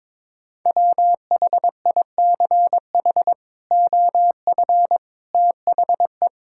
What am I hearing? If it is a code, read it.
Morse code, heard as a keyed tone, about 22 wpm, 700 Hz. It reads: WHICH OF THE
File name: practice / 22wpm / Sentences with 8-Word Phrases - 22wpm